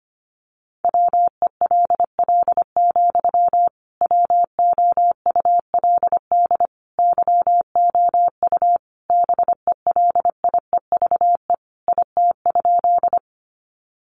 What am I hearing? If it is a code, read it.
Morse code, heard as a keyed tone, about 25 wpm, 700 Hz. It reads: WELL, WOULD YOU BELIEVE IT?
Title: Sentences from Top 1000 Words 25wpm